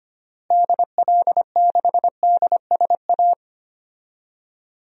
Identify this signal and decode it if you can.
Morse code, heard as a keyed tone, about 25 wpm, 700 Hz. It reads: DL6DSA
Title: DXCC-100 Call Signs - 6 Characters - Repeated 2x - 25wpm